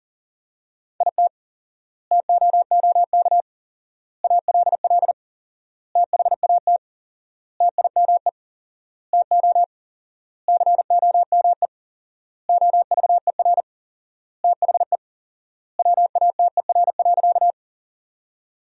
Morse code, heard as a keyed tone, about 40 wpm, 700 Hz. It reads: IT TOOK ALL THAT TIME TO COME OVER THE WATER.
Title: 4x Word-Spacing - Sentences from Top 300 Words - 40wpm